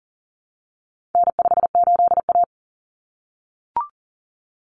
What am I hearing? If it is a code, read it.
Morse code, heard as a keyed tone, about 40 wpm, 700 Hz. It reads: N58A
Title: Common Cut Numbers - 4-Digit 40wpm